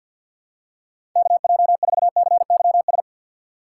Morse code, heard as a keyed tone, about 46 wpm, 700 Hz. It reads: KJ4CYS
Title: USA Call Signs - 4x Character Spacing - 50wpm